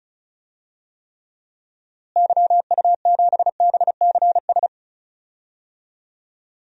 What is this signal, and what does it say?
Morse code, heard as a keyed tone, about 36 wpm, 700 Hz. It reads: YU7BCS
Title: DXCC-100 Call Signs - 6 Characters - Repeated 2x - 35wpm